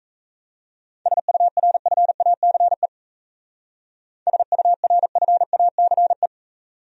Morse code, heard as a keyed tone, about 34 wpm, 700 Hz. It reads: SURFACE SURFACE
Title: Speed-Racing Top 500 Words 28wpm